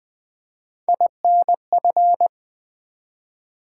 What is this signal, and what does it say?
Morse code, heard as a keyed tone, about 20 wpm, 700 Hz. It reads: INF
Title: Build A Word - 1 Letter At A Time - 5-Letter Words - 20wpm